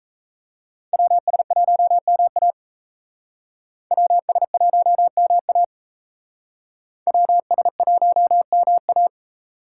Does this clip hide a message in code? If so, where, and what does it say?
Yes: Morse code from 0.9 to 9.1 s, heard as a keyed tone, about 36 wpm, 700 Hz. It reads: WS1MA WS1MA WS1MA